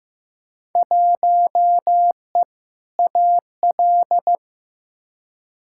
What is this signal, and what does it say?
Morse code, heard as a keyed tone, about 15 wpm, 700 Hz. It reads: 1E AL
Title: Field Day Exchanges - 15wpm